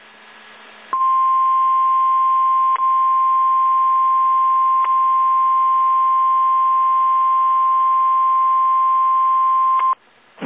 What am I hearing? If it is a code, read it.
Emergency Alert System (SAME) tones (the 1050 Hz attention tone). It is heard from 0.9 to 10.0 s.